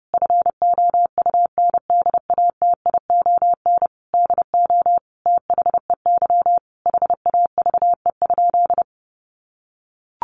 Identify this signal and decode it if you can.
Morse code, heard as a keyed tone, about 28 wpm, 700 Hz. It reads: FOUNDATION DO THEY HAVE?
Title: Sentences from Top 1200 Words 30wpm